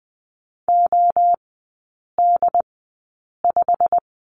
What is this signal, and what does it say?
Morse code, heard as a keyed tone, about 20 wpm, spaced as for 8 wpm, 700 Hz. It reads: OD5
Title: 087 Lesson 15e - Number 5 - Three Characters 20wpm